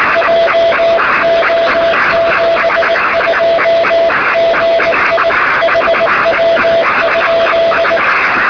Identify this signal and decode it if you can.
Morse code, heard as a keyed tone, about 19 wpm, 670 Hz, with noise. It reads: JO72GI HW?
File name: sm7lcb / 1739z dk3wg 23cm